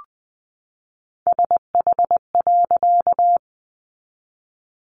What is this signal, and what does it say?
Morse code, heard as a keyed tone, about 20 wpm, 700 Hz. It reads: SH.
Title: A Sound Of Thunder 20wpm